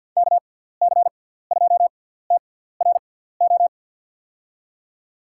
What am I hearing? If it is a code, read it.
Morse code, heard as a keyed tone, about 47 wpm, spaced as for 17 wpm, 700 Hz. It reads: KC2TRO
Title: USA Call Signs - 6 Characters - 6x Character Spacing - 50wpm